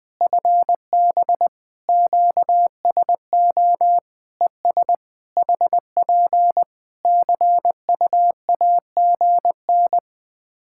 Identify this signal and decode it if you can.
Morse code, heard as a keyed tone, about 21 wpm, 700 Hz. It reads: FB QSO ES HP CUAGN
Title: Practice QSO - Exchange 4 - Ending - 20wpm